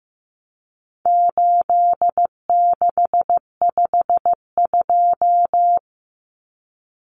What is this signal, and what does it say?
Morse code, heard as a keyed tone, about 15 wpm, 700 Hz. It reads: 8652
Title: Common Cut Numbers - 4-Digit 15wpm